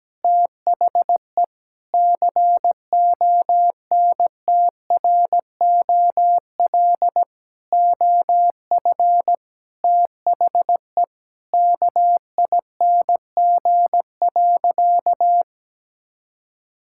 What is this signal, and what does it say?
Morse code, heard as a keyed tone, about 17 wpm, 700 Hz. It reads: THE CONTROL OF THE KING.
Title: Sentences from Top 1000 Words - Review Part 5 - 17wpm